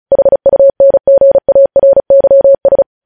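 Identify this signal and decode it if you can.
Morse code, heard as a keyed tone, about 36 wpm, 550 Hz. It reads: HUNGARYS
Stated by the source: News headlines sent in Morse at 35 WPM.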